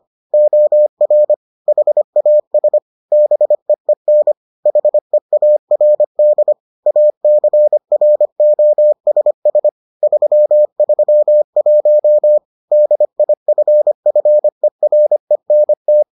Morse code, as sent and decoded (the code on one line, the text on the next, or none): --- .-. / .... .- ... / -... . -. / .... .- .-. -.. / .- -.-. .-. --- ... ... / ...-- ...-- .---- / -.. .. ..-. ..-. .-. -. -
OR HAS BEEN HEARD ACROSS 331 DIFFERENT